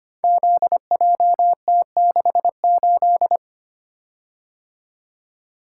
Morse code, heard as a keyed tone, about 25 wpm, 700 Hz. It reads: ZJT68
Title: Code Groups - 5 Letter:Numbers - 25wpm